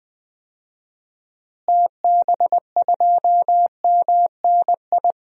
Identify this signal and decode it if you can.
Morse code, heard as a keyed tone, about 20 wpm, 700 Hz. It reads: TB2MNI
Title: DXCC-100 Call Signs - 6 Characters - Repeated 2x - 20wpm